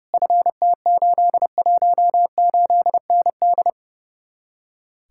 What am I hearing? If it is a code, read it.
Morse code, heard as a keyed tone, about 37 wpm, 700 Hz. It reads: FT818ND
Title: Speed-Racing Common QSO Phrases Encore - 30wpm